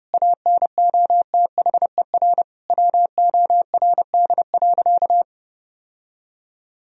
Morse code, heard as a keyed tone, about 30 wpm, 700 Hz. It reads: ANOTHER WORD.
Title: Sentences from Top 1000 Words - Review Part 3 - 30wpm